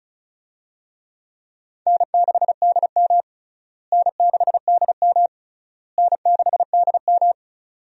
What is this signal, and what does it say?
Morse code, heard as a keyed tone, about 36 wpm, 700 Hz. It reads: N6DM N6DM N6DM
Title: USA Call Signs - 4 Characters - Repeated 3x - 35wpm